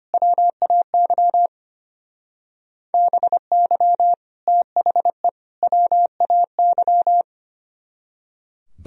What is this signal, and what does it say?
Morse code, heard as a keyed tone, about 31 wpm, 700 Hz. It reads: WAY BY THE WAY
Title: Speed-Racing - Sets of 3 Words 25wpm